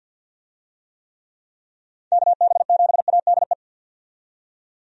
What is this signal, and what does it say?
Morse code, heard as a keyed tone, about 51 wpm, 700 Hz. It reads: KB7ADE
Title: USA Call Signs - 6 Characters - 50wpm